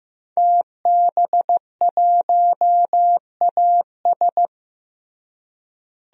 Morse code, heard as a keyed tone, about 15 wpm, 700 Hz. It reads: TB1AS
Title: DXCC-100 Call Signs - 5 Characters - Repeated 2x - 15wpm